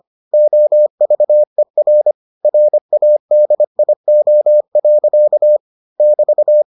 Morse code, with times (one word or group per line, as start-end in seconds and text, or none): 0.3-2.1 OVER
2.5-5.6 RADIO.
6.0-6.6 <BT>